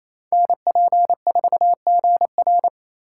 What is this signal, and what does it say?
Morse code, heard as a keyed tone, about 28 wpm, 700 Hz. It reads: NP4GR
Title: USA Call Signs - 5 Characters - 2x Character Spacing - 28wpm